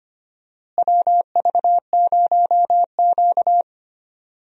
Morse code, heard as a keyed tone, about 25 wpm, 700 Hz. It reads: WV0Q
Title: USA Call Signs - 4 Characters - Repeated 3x - 25wpm